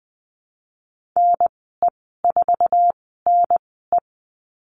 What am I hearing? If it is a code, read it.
Morse code, heard as a keyed tone, about 19 wpm, spaced as for 14 wpm, 700 Hz. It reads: NE4NE
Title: 082 Lesson 14g - Number 2 - Call Signs 20wpm